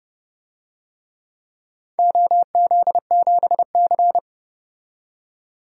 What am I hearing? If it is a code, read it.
Morse code, heard as a keyed tone, about 30 wpm, 700 Hz. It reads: OZ7C
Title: DXCC-100 Call Signs - 4 Characters - 30wpm